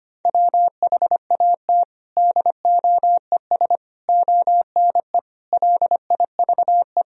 Morse code, heard as a keyed tone, about 25 wpm, 700 Hz. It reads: WHAT DOES ONE LIVE
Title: Sentences from Top 200 Words and Abbreviations 25wpm